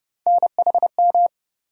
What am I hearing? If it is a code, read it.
Morse code, heard as a keyed tone, about 30 wpm, 700 Hz. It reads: NHM
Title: USA Call Sign Postfixes - 3 Letters 30wpm